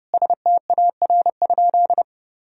Morse code, heard as a keyed tone, about 30 wpm, 700 Hz. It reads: STAR?
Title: Sentences from Top 1000 Words - Review Part 2 - 30wpm